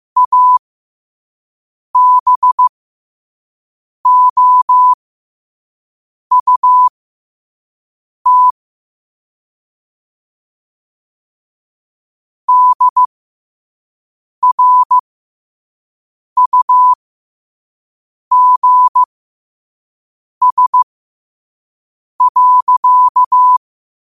Morse code, heard as a keyed tone, about 15 wpm, spaced as for 5 wpm, 1000 Hz. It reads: ABOUT DRUGS.
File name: qotdfiles / 5wpm